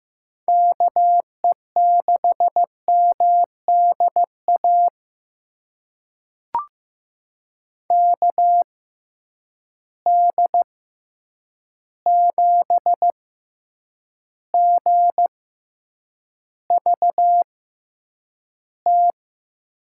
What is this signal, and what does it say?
Morse code, heard as a keyed tone, about 15 wpm, spaced as for 8 wpm, 700 Hz. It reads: KE6MDA KD7GVT
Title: USA Call Signs - 6 Characters - 6x Character Spacing - 15wpm